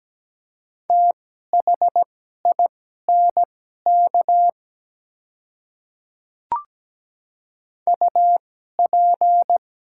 Morse code, heard as a keyed tone, about 17 wpm, spaced as for 12 wpm, 700 Hz. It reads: THINK UP
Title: 2x Character Spacing - Top 200 Words - 17wpm